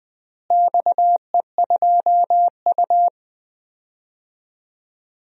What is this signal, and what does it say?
Morse code, heard as a keyed tone, about 20 wpm, 700 Hz. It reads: XE2U
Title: DXCC-100 Call Signs - 4 Characters - Repeated 3x - 20wpm